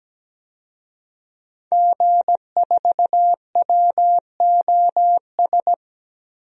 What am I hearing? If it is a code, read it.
Morse code, heard as a keyed tone, about 17 wpm, 700 Hz. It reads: G4WOS
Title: DXCC-100 Call Signs - 5 Characters - 17wpm